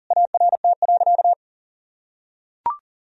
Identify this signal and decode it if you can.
Morse code, heard as a keyed tone, about 40 wpm, 700 Hz. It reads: ART.
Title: Sentences from Top 1000 Words - Review Part 3 - 40wpm